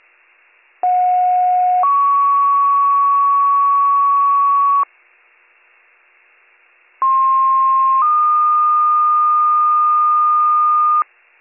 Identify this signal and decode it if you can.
Two-tone page, heard as a one-second tone, then a three-second tone, about 730 then 1120 Hz, about 1060 then 1220 Hz. It is heard from 0.8 to 11.0 s.